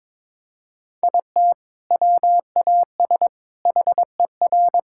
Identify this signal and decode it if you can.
Morse code, heard as a keyed tone, about 22 wpm, 700 Hz. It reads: IT WAS HER
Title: Sentences from Top 1000 Words - Review Part 6 - 22wpm